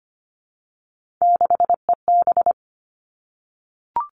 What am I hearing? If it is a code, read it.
Morse code, heard as a keyed tone, about 25 wpm, 700 Hz. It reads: 6EB
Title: Three Challenging Letter-Number Combinations 25wpm